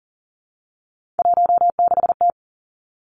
Morse code, heard as a keyed tone, about 38 wpm, 700 Hz. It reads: 16T
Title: Three Challenging Letter-Number Combinations 40wpm